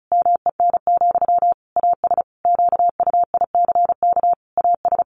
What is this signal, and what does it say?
Morse code, heard as a keyed tone, about 34 wpm, 700 Hz. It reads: MEN, AS QUICK AS
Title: Sentences from Top 500 Words 35wpm